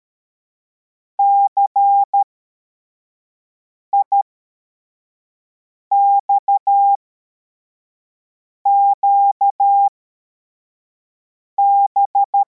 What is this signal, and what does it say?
Morse code, heard as a keyed tone, about 13 wpm, spaced as for 4 wpm, 800 Hz. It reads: CIXQB